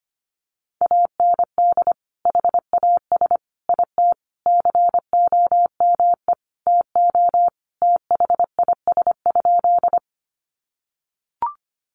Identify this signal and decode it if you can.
Morse code, heard as a keyed tone, about 25 wpm, 700 Hz. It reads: AND HAS IT COME TO THIS?
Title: Sentences from Top 100 Words 25wpm